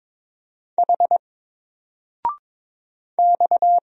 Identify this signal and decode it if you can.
Morse code, heard as a keyed tone, about 22 wpm, 700 Hz. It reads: H X
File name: Single-Letter-Number-22wpm